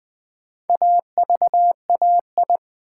Morse code, heard as a keyed tone, about 20 wpm, 700 Hz. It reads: AVAI
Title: Build A Word - 4 Letters At A Time - 12-Letter Words - 20wpm